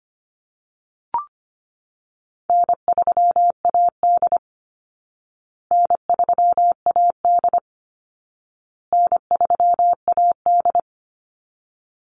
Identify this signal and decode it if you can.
Morse code, heard as a keyed tone, about 25 wpm, 700 Hz. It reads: N3AD N3AD N3AD
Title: Easy Call Signs Repeated 3x - 25wpm